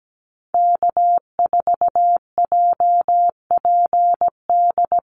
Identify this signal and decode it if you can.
Morse code, heard as a keyed tone, about 17 wpm, 700 Hz. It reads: K4JPD
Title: Call Signs Repeated 2x - 17wpm